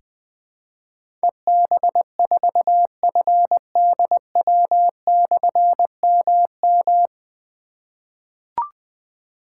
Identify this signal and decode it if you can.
Morse code, heard as a keyed tone, about 20 wpm, 700 Hz. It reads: EB4FDW/MM
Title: Difficult Call Signs 20wpm